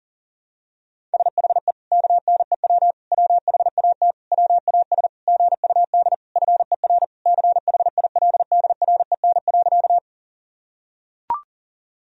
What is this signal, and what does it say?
Morse code, heard as a keyed tone, about 37 wpm, 700 Hz. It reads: SHE KNEW WHAT WAS GUD FER CHILDREN.
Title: Sentences from Top 400 Words and Abbreviations 40wpm